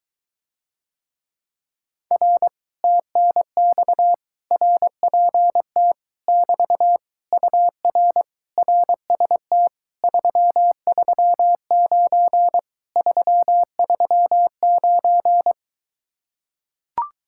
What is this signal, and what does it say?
Morse code, heard as a keyed tone, about 24 wpm, 700 Hz. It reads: R TNX RPT <BT> UR RST 339 339